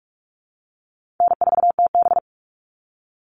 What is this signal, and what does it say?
Morse code, heard as a keyed tone, about 44 wpm, 700 Hz. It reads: N4TB
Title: Easy Call Signs Repeated 2x - 45wpm